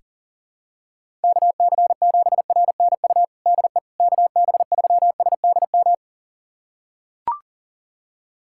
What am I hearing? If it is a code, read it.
Morse code, heard as a keyed tone, about 40 wpm, 700 Hz. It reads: KC7RNU DE KB3SDM
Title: USA Callsign DE Callsign - 6 Characters - 40wpm